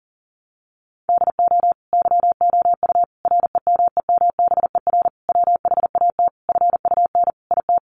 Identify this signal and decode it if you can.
Morse code, heard as a keyed tone, about 40 wpm, 700 Hz. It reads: DO YOU REMEMBER WHAT FUN IT
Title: Sentences from Top 700 Words 40wpm